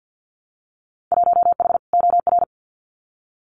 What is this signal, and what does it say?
Morse code, heard as a keyed tone, about 48 wpm, 700 Hz. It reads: 1H OR
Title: Winter Field Day Exchanges 50wpm